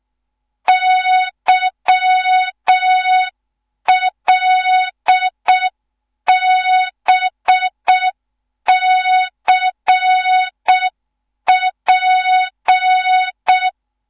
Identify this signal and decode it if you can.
Morse code, heard as a keyed tone, about 6 wpm, 3700 Hz. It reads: YLBCP
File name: Morze-HA2MN / hang / 14 280 HA2MN